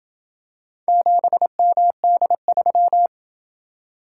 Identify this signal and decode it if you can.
Morse code, heard as a keyed tone, about 28 wpm, 700 Hz. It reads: 7MD3